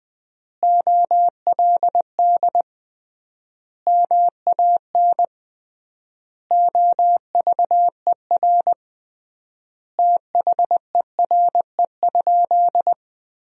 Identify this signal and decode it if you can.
Morse code, heard as a keyed tone, about 21 wpm, 700 Hz. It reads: OLD MAN OVER THERE?
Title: 3x Word-Spacing - Sentences from Top 200 Words - 20wpm